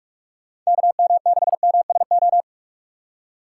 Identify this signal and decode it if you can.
Morse code, heard as a keyed tone, about 46 wpm, 700 Hz. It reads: KM6MSO
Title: USA Call Signs - 6 Characters - 2x Character Spacing - 45wpm